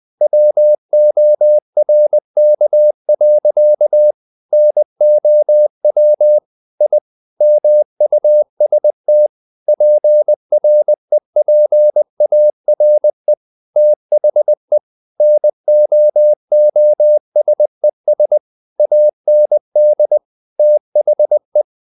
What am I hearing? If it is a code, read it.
Morse code, heard as a keyed tone, about 20 wpm, 600 Hz. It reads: WORK. NOW I MUST PREPARE THE NOOSES AND THE
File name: the old man and the sea part 2 20wpm